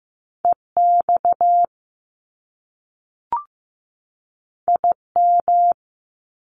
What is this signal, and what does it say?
Morse code, heard as a keyed tone, about 15 wpm, 700 Hz. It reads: EX IM
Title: Top Two Letter Combinations in Words 15wpm